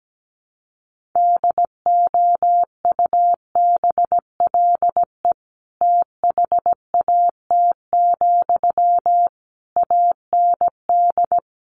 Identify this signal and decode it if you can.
Morse code, heard as a keyed tone, about 17 wpm, 700 Hz. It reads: DOUBLE THAT, AND
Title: Sentences from Top 1400 Words 17wpm